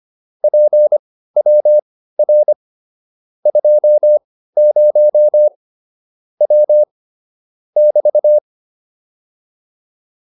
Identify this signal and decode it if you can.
Morse code, heard as a keyed tone, about 25 wpm, spaced as for 15 wpm, 600 Hz. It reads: PWR 20 W <BT>
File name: QSO403 15